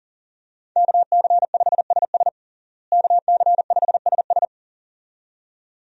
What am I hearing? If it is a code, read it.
Morse code, heard as a keyed tone, about 40 wpm, 700 Hz. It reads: KC5SS KC5SS
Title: USA Call Signs - 5 Characters - Repeated 2x - 40wpm